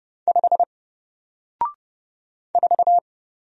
Morse code, heard as a keyed tone, about 30 wpm, 700 Hz. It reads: 5 4
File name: Single-Letter-Number-30wpm